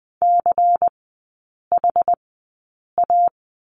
Morse code, heard as a keyed tone, about 20 wpm, spaced as for 8 wpm, 700 Hz. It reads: CHA
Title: 093 Lesson 16e - Letter c - Three Characters 20wpm